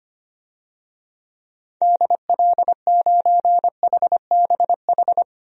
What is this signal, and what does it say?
Morse code, heard as a keyed tone, about 25 wpm, 700 Hz. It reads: DL9HBH
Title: DXCC-100 Call Signs - 6 Characters - 25wpm